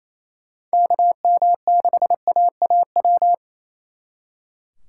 Morse code, heard as a keyed tone, about 36 wpm, 700 Hz. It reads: KM6AAW